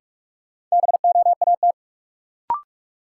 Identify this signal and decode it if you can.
Morse code, heard as a keyed tone, about 43 wpm, 700 Hz. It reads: BOAT
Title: Sentences from Top 500 Words - New Word Drill - 45wpm